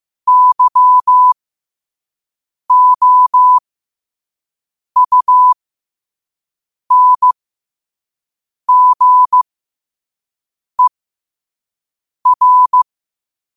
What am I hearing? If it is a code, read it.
Morse code, heard as a keyed tone, about 15 wpm, spaced as for 5 wpm, 1000 Hz. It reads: YOUNGER
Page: 5 WPM morse code quotes for Wed, 30 Jul 2025.